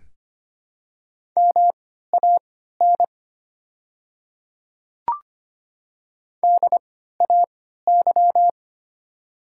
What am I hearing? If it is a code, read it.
Morse code, heard as a keyed tone, about 25 wpm, spaced as for 14 wpm, 700 Hz. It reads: MAN DAY
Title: 3x Character Spacing - Top 200 Words - 25wpm